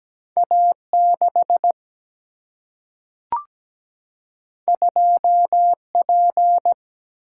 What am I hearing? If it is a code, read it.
Morse code, heard as a keyed tone, about 17 wpm, 700 Hz. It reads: A6 2P
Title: Single-Letter Single-Number 17wpm